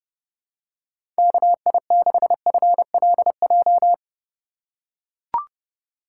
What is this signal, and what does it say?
Morse code, heard as a keyed tone, about 30 wpm, 700 Hz. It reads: KI6FLJ
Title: USA Call Signs - 6 Characters - 2x Character Spacing - 30wpm